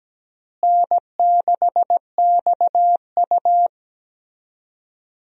Morse code, heard as a keyed tone, about 17 wpm, 700 Hz. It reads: N6XU
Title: USA Call Signs - 4 Characters - Repeated 3x - 17wpm